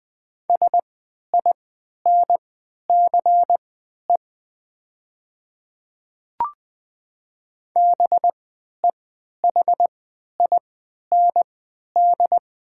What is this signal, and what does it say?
Morse code, heard as a keyed tone, about 20 wpm, spaced as for 11 wpm, 700 Hz. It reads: SINCE BEHIND
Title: 3x Character Spacing - Top 500 Words - 20wpm